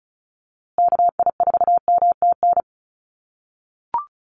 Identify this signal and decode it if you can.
Morse code, heard as a keyed tone, about 34 wpm, 700 Hz. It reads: KI4MTN
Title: USA Call Signs Repeated 2x 35wpm